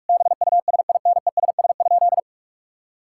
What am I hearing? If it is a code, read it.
Morse code, heard as a keyed tone, about 46 wpm, 700 Hz. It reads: BUSINESS?
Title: Sentences from Top 1000 Words - Review Part 6 - 45wpm